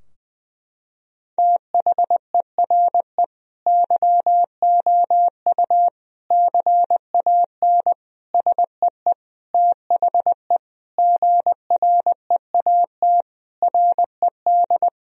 Morse code, heard as a keyed tone, about 20 wpm, 700 Hz. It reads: THERE YOU CAN SEE THE GREAT RED
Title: Sentences from Top 1000 Words - Review Part 2 - 20wpm